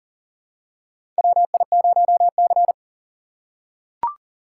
Morse code, heard as a keyed tone, about 37 wpm, 700 Hz. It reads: WI0C
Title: USA Call Signs - 4 Characters - 2x Character Spacing - 40wpm